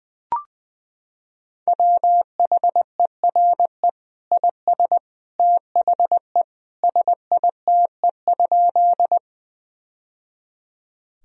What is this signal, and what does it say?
Morse code, heard as a keyed tone, about 21 wpm, 700 Hz. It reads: WHERE IS THE SITE?
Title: Sentences from Top 1000 Words - Review Part 4 - 20wpm